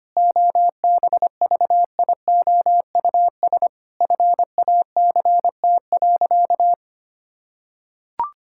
Morse code, heard as a keyed tone, about 25 wpm, 700 Hz. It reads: OBVIOUS FACT.
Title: Sentences from Top 1600 Words 25wpm